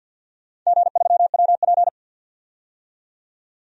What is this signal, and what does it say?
Morse code, heard as a keyed tone, about 48 wpm, 700 Hz. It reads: G3WP
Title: DXCC-100 Call Signs - 4 Characters - 50wpm